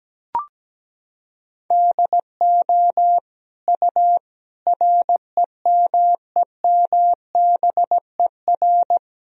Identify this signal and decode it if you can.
Morse code, heard as a keyed tone, about 17 wpm, 700 Hz. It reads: DO U REMEMBER
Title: Sentences from Top 400 Words and Abbreviations 17wpm